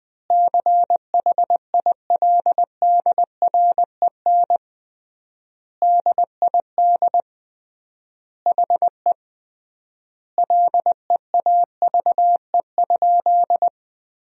Morse code, heard as a keyed tone, about 21 wpm, 700 Hz. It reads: CHILDREN DID HE LEAVE?
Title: 3x Word-Spacing - Sentences from Top 400 Words - 20wpm